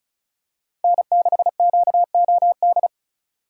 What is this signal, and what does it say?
Morse code, heard as a keyed tone, about 36 wpm, 700 Hz. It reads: N6QOD
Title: USA Call Signs - 5 Characters - 2x Character Spacing - 35wpm